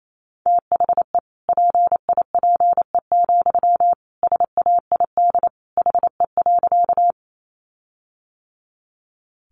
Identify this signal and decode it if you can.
Morse code, heard as a keyed tone, about 28 wpm, 700 Hz. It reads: THE PIPE, SAID HE.
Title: Sentences from Top 1500 Words 28wpm